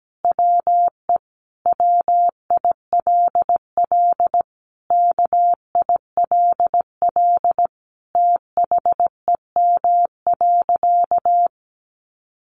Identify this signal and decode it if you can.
Morse code, heard as a keyed tone, about 17 wpm, 700 Hz. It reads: WE WILL KILL THEM.
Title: Sentences from Top 1400 Words 17wpm